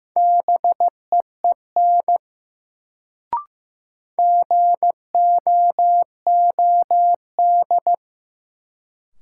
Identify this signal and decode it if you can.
Morse code, heard as a keyed tone, about 15 wpm, 700 Hz. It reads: BEEN GOOD
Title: Sets of 2 Words - New Word Drill - 15wpm